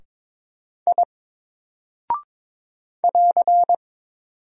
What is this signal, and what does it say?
Morse code, heard as a keyed tone, about 22 wpm, 700 Hz. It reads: I <AR>
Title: Single Character (Letter:Number:Punctuation:Prosign) 22wpm